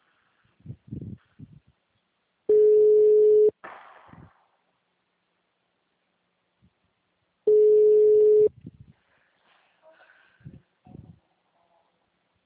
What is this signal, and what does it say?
Ringback tone (Europe, 425 Hz), with speech.
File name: out- -2017- - - .16416